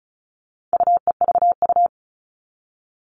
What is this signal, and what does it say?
Morse code, heard as a keyed tone, about 34 wpm, 700 Hz. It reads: UEVU
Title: All Cut Numbers - 4-Digit 35wpm